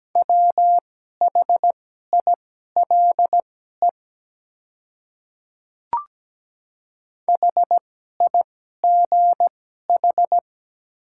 Morse code, heard as a keyed tone, about 17 wpm, spaced as for 12 wpm, 700 Hz. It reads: WHILE HIGH
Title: 2x Character Spacing - Top 300 Words - 17wpm